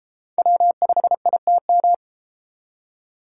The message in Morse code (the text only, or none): W5ITM